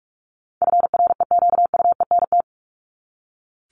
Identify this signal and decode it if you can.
Morse code, heard as a keyed tone, about 44 wpm, 700 Hz. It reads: FREQUENT
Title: Top 1251-1500 Words 45wpm